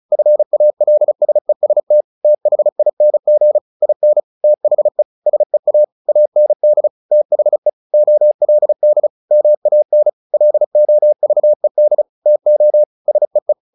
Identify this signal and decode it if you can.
Morse code, heard as a keyed tone, about 34 wpm, 600 Hz. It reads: FALSEST THING IN THE SEA AND THE OLD MAN LOVED TO SEE
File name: the old man and the sea / the old man and the sea 1 35wpm